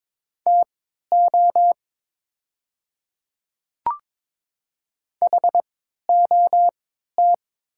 Morse code, heard as a keyed tone, about 22 wpm, spaced as for 12 wpm, 690 Hz. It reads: TO HOT
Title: 3x Character Spacing - Top 100 Words - 22wpm